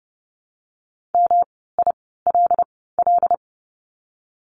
Morse code, heard as a keyed tone, about 30 wpm, spaced as for 17 wpm, 700 Hz. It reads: MILL
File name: course / 30:10wpm / 108 Lesson 18f - Letter m - Words 30wpm